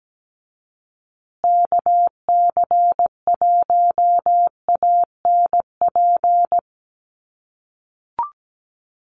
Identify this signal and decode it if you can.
Morse code, heard as a keyed tone, about 17 wpm, 700 Hz. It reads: KC1ANP
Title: USA Call Signs Repeated 2x 17wpm